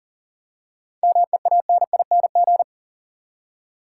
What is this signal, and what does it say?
Morse code, heard as a keyed tone, about 37 wpm, 700 Hz. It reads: MEANING
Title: Seven Letter Words - 40wpm